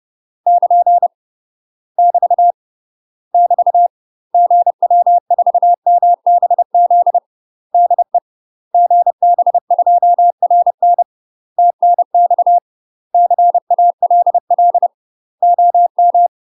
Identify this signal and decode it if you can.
Morse code, heard as a keyed tone, about 30 wpm, 700 Hz. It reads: <KN> <BT> <BT> GW4MBZ DE GB2RN TNX CALL OM
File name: UK MASTER QSOs TRAINING / 30WPM / UK MASTER QSOs 0018